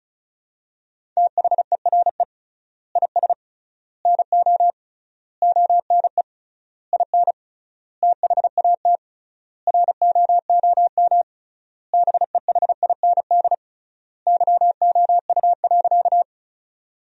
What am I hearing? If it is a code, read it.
Morse code, heard as a keyed tone, about 34 wpm, 700 Hz. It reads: THERE IS NO ONE IN THAT ROOM BEHIND YOU.
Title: 3x Word-Spacing - Sentences from Top 500 Words - 35wpm